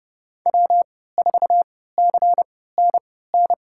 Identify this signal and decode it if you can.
Morse code, heard as a keyed tone, about 32 wpm, spaced as for 17 wpm, 700 Hz. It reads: W4CNN
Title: USA Call Signs - 3x Character Spacing - 30wpm